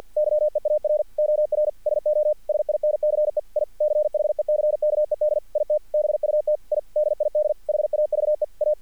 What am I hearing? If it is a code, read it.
Morse code, heard as a keyed tone, about 50 wpm, 600 Hz. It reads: YEAR OR SO SINCE I CHECKED IT BUT I DID HAVE A
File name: archived / CRSnet